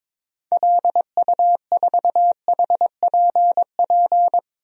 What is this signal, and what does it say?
Morse code, heard as a keyed tone, about 21 wpm, 700 Hz. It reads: LU4HPP
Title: DXCC-100 Call Signs - 6 Characters - 22wpm